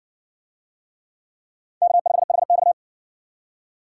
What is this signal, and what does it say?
Morse code, heard as a keyed tone, about 60 wpm, 700 Hz. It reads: K5HX